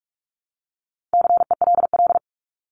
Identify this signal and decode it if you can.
Morse code, heard as a keyed tone, about 44 wpm, 700 Hz. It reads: CELL
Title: ES:S Suffix - Top 100 Words with Base Words 45wpm